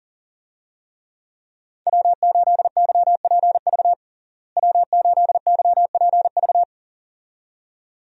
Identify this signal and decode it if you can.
Morse code, heard as a keyed tone, about 40 wpm, 700 Hz. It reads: W8YPV W8YPV
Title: USA Call Signs - 5 Characters - Repeated 2x - 40wpm